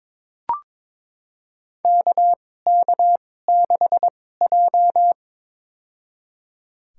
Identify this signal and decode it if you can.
Morse code, heard as a keyed tone, about 22 wpm, spaced as for 16 wpm, 700 Hz. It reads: KK6J
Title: USA Call Signs - 4 Characters - 2x Character Spacing - 22wpm